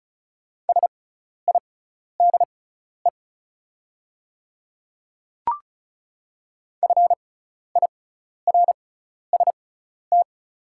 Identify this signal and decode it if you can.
Morse code, heard as a keyed tone, about 34 wpm, spaced as for 12 wpm, 690 Hz. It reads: SIDE FIRST
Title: 6x Character Spacing - Top 200 Words - 35wpm